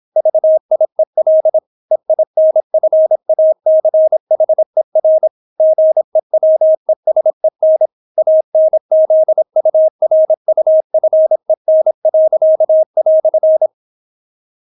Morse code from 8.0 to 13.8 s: ANZURUFEN."